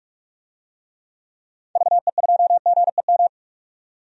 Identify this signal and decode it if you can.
Morse code, heard as a keyed tone, about 44 wpm, 690 Hz. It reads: VE2GEM
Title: DXCC-100 Call Signs - 6 Characters - 45wpm